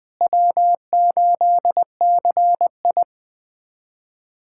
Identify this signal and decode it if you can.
Morse code, heard as a keyed tone, about 20 wpm, 700 Hz. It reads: W8CI
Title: USA Callsign DE Callsign - 4 Characters - 20wpm